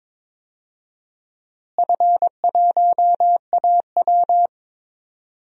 Morse code, heard as a keyed tone, about 22 wpm, 700 Hz. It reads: F1AW
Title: DXCC-100 Call Signs - 4 Characters - Repeated 2x - 22wpm